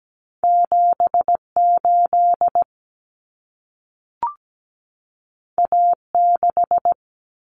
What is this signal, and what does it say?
Morse code, heard as a keyed tone, about 17 wpm, 700 Hz. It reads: 78 A6
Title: Common Cut Numbers - 2-Digit 17wpm